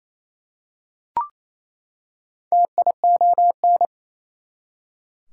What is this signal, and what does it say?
Morse code, heard as a keyed tone, about 28 wpm, 700 Hz. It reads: TION
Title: Prefixes and Suffixes 28wpm